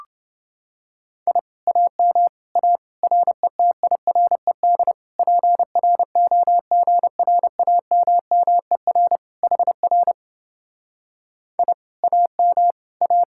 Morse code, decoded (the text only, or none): I AM A RETIRED PROGRAMMER HR I AM A